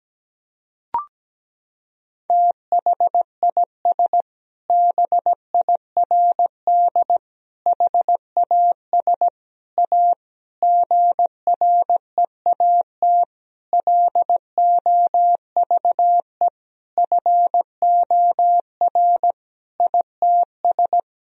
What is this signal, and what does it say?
Morse code, heard as a keyed tone, about 17 wpm, 700 Hz. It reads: THIS BIRD HAS A GREAT LOVE FOR ITS
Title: Sentences from Top 1000 Words - Review Part 2 - 17wpm